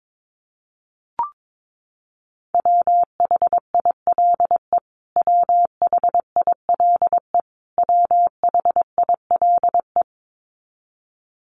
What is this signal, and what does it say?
Morse code, heard as a keyed tone, about 22 wpm, 700 Hz. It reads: WHILE WHILE WHILE
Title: Top 300 Words Repeated 22wpm